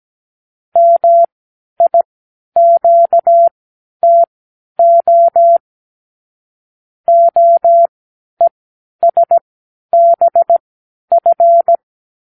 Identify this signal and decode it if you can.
Morse code, heard as a keyed tone, about 17 wpm, spaced as for 10 wpm, 680 Hz. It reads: MIQTO OESBF
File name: k7qo / Lesson 36 (A-X)